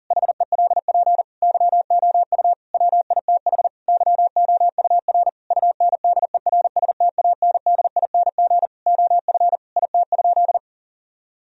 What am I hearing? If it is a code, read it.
Morse code, heard as a keyed tone, about 40 wpm, 700 Hz. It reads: HELP YOU WITH YOUR UNDERSTANDING OF IT?
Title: Sentences from Top 1000 Words - Review Part 3 - 40wpm